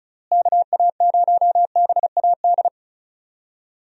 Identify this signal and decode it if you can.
Morse code, heard as a keyed tone, about 36 wpm, 700 Hz. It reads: KA0BAD
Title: USA Call Signs - 6 Characters - Repeated 2x - 35wpm